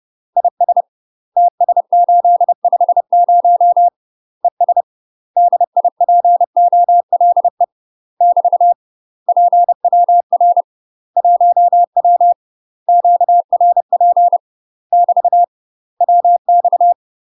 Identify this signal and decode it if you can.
Morse code, heard as a keyed tone, about 30 wpm, 700 Hz. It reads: IS TS850 ES DIPOLE <BT> PWR 1W QRP <BT> WX
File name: UK MASTER QSOs TRAINING / 30WPM / UK MASTER QSOs 0016